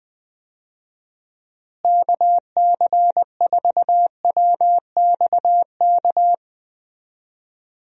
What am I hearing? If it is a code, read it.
Morse code, heard as a keyed tone, about 21 wpm, 700 Hz. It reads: KC4WXK
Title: USA Call Signs - 6 Characters - Repeated 3x - 20wpm